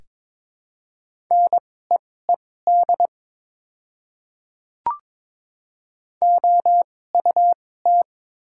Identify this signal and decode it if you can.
Morse code, heard as a keyed tone, about 22 wpm, spaced as for 16 wpm, 700 Hz. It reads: NEED OUT
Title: 2x Character Spacing - Top 200 Words - 22wpm